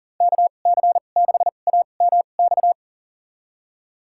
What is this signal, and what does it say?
Morse code, heard as a keyed tone, about 40 wpm, spaced as for 29 wpm, 700 Hz. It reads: KC6AMX
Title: USA Call Signs - 2x Character Spacing - 40wpm